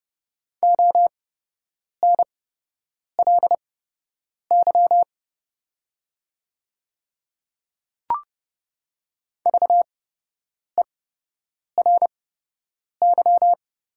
Morse code, heard as a keyed tone, about 30 wpm, spaced as for 8 wpm, 700 Hz. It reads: ONLY VERY
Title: 8x Character Spacing - Top 200 Words - 30wpm